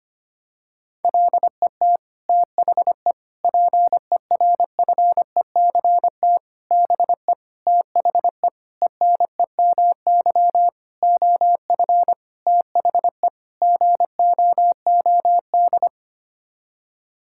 Morse code, heard as a keyed tone, about 25 wpm, 700 Hz. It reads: LET THE PERFECT BE THE ENEMY OF THE GOOD
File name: Sets of 9 Words - 25wpm